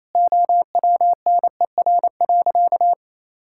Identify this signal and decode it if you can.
Morse code, heard as a keyed tone, about 28 wpm, 700 Hz. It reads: OWNER.
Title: Sentences from Top 1600 Words 28wpm